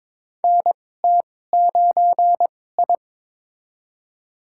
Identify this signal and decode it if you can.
Morse code, heard as a keyed tone, about 22 wpm, spaced as for 16 wpm, 700 Hz. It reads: NT9I
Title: USA Call Signs - 2x Character Spacing - 22wpm